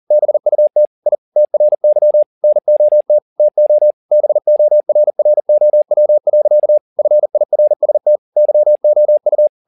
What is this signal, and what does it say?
Morse code, heard as a keyed tone, about 40 wpm, 600 Hz. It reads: BUT I TRY NOT TO BORROW. FIRST YOU